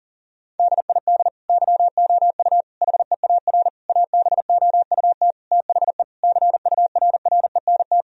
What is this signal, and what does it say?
Morse code, heard as a keyed tone, about 40 wpm, 700 Hz. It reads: DID YOU HEAR ABOUT THE CURRENT
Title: Sentences from Top 1000 Words - Review Part 4 - 40wpm